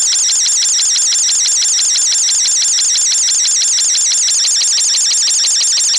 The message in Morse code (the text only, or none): none